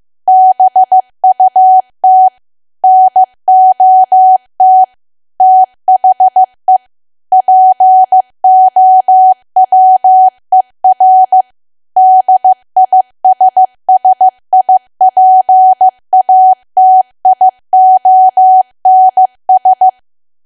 Morse code, heard as a keyed tone, about 15 wpm, 750 Hz. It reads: BUT NOT THE POWER DISSIPATIONS
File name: cwa / W1AW-Level2 / 15WPM